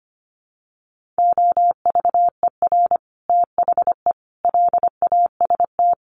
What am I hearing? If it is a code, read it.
Morse code, heard as a keyed tone, about 25 wpm, 700 Hz. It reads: OVER THE LAST
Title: Sets of 5 Words Encore 25wpm